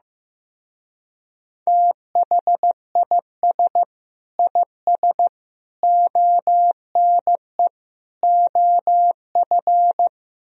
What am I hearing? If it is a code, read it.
Morse code, heard as a keyed tone, about 15 wpm, 700 Hz. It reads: THIS IS ONE OF
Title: Speed-Racing - Sets of 4 Words 15wpm